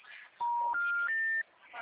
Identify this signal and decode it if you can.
Special information tone.